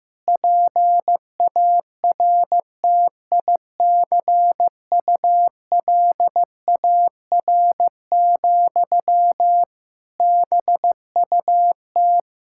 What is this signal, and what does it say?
Morse code, heard as a keyed tone, about 15 wpm, 700 Hz. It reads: PARTICULAR, BUT
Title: Sentences from Top 1000 Words - Review Part 5 - 15wpm